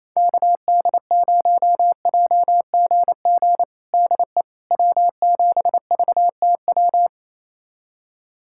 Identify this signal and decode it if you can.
Morse code, heard as a keyed tone, about 28 wpm, 700 Hz. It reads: KD0JGG DE W7VTW
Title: USA Callsign DE Callsign 28wpm